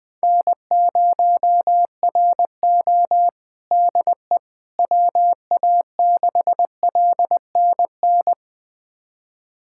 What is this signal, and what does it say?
Morse code, heard as a keyed tone, about 20 wpm, 700 Hz. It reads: N0RO DE WA6LNN
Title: USA Callsign DE Callsign 20wpm